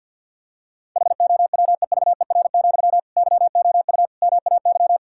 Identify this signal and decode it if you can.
Morse code, heard as a keyed tone, about 52 wpm, 700 Hz. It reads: HOWEVER, YOU MAY
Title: Sentences from Top 1600 Words 50wpm